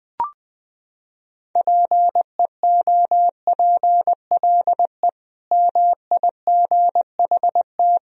Morse code, heard as a keyed tone, about 20 wpm, 700 Hz. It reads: PEOPLE MIGHT
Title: Sentences from Top 500 Words and Abbreviations 20wpm